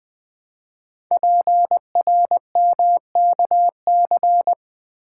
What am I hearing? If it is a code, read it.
Morse code, heard as a keyed tone, about 20 wpm, 700 Hz. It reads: PRMKC
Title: Code Groups - 5 Letters - 20wpm